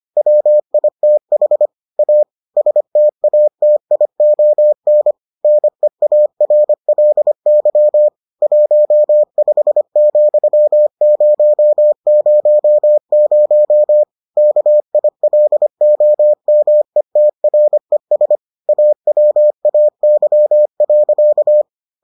Morse code, heard as a keyed tone, about 25 wpm, 600 Hz. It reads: WITH A STATION NEARLY 15,000 KILOMETRES AWAY.